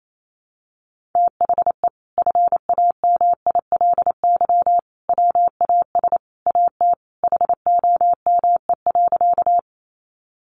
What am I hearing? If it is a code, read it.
Morse code, heard as a keyed tone, about 28 wpm, 700 Hz. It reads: THE FAMILY WAS AT HOME.
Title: Sentences from Top 400 Words 28wpm